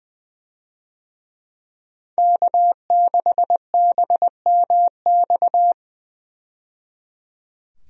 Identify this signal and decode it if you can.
Morse code, heard as a keyed tone, about 20 wpm, 700 Hz. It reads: K6BMX
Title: USA Call Signs - 5 Characters - 20wpm